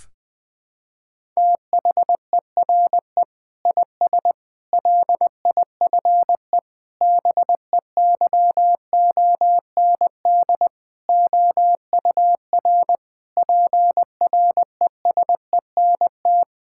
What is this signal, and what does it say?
Morse code, heard as a keyed tone, about 20 wpm, 700 Hz. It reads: THERE IS LIFE BEYOND OUR PRESENT
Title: Sentences from Top 1000 Words - Review Part 6 - 20wpm